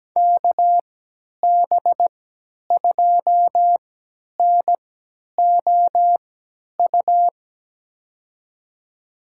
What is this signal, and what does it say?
Morse code, heard as a keyed tone, about 17 wpm, spaced as for 10 wpm, 700 Hz. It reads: KB2NOU
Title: USA Call Signs - 6 Characters - 3x Character Spacing - 17wpm